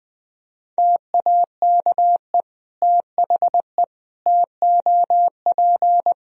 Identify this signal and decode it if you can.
Morse code, heard as a keyed tone, about 21 wpm, 700 Hz. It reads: TAKE THE TOP
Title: Sentences from Top 1000 Words - Review Part 2 - 20wpm